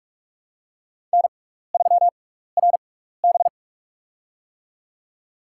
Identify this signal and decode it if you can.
Morse code, heard as a keyed tone, about 46 wpm, spaced as for 15 wpm, 700 Hz. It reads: N3RB
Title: USA Call Signs - 4 Characters - 6x Character Spacing - 45wpm